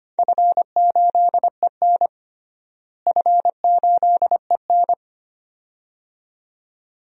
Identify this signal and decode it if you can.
Morse code, heard as a keyed tone, about 25 wpm, 700 Hz. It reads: F8EN F8EN
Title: DXCC-100 Call Signs - 4 Characters - Repeated 2x - 25wpm